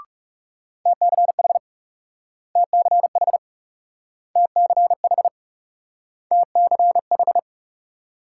Morse code, heard as a keyed tone, about 41 wpm, 700 Hz. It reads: TCH TCH TCH TCH